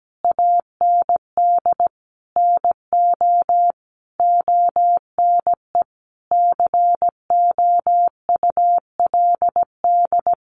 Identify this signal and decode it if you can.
Morse code, heard as a keyed tone, about 17 wpm, 700 Hz. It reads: AND NO ONE COULD
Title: Sentences from Top 1400 Words 17wpm